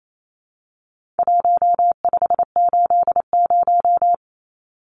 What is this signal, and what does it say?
Morse code, heard as a keyed tone, about 28 wpm, 700 Hz. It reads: 1580
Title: Numbers - Contest Style - 4-Digit 28wpm